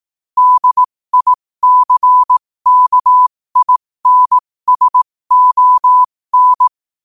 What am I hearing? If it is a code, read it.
Morse code, heard as a keyed tone, about 18 wpm, spaced as for 15 wpm, 1000 Hz. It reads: DICKINSON
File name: qotdfiles / 15wpm